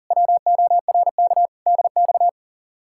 Morse code, heard as a keyed tone, about 47 wpm, 700 Hz. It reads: WORK DX
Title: Speed-Racing Common QSO Phrases Encore - 40wpm